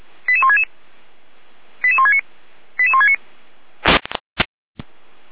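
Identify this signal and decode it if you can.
Five-tone selective call (ZVEI), heard as quick runs of five tones, tones of about 70 ms. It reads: 88166 88169 88169
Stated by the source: ZVEI five-tone call 88167 received by SDR.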